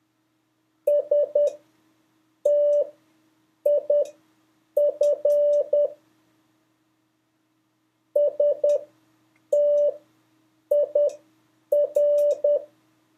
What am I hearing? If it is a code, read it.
Morse code, heard as a keyed tone, about 10 wpm, spaced as for 7 wpm, 580 Hz. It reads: STIF STIR
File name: 3of5-words-2